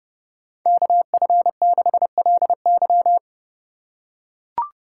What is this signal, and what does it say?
Morse code, heard as a keyed tone, about 30 wpm, 700 Hz. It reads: KF6LY
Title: USA Call Signs - 5 Characters - 3x Character Spacing - 30wpm